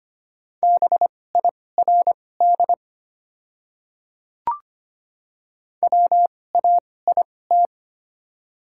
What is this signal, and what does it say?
Morse code, heard as a keyed tone, about 25 wpm, spaced as for 18 wpm, 700 Hz. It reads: BIRD WAIT
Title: 2x Character Spacing - Top 500 Words - 25wpm